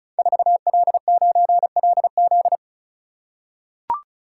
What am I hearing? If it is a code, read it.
Morse code, heard as a keyed tone, about 36 wpm, 700 Hz. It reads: 4L9LZ
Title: Code Groups - 5 Letter:Numbers - 35wpm